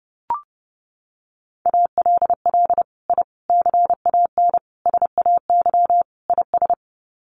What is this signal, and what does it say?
Morse code, heard as a keyed tone, about 30 wpm, 700 Hz. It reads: ALL I CAN SAY IS
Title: Sets of 5 Words Encore2 30wpm